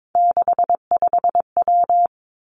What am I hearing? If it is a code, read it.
Morse code, heard as a keyed tone, about 22 wpm, 700 Hz. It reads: 65W
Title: Three Challenging Letter-Number Combinations 22wpm